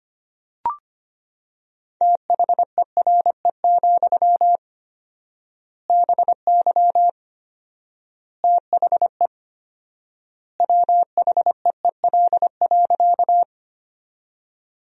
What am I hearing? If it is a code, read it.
Morse code, heard as a keyed tone, about 25 wpm, 700 Hz. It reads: THERE, BY THE WHEEL.
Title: 4x Word-Spacing - Sentences from Top 500 Words - 25wpm